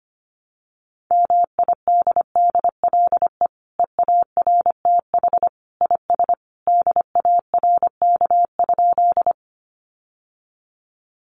Morse code, heard as a keyed tone, about 25 wpm, 700 Hz. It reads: MIDDLE EARTH IS DARK?
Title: Sentences from Top 900 Words 25wpm